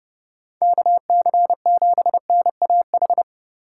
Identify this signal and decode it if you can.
Morse code, heard as a keyed tone, about 38 wpm, 700 Hz. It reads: KC7NAH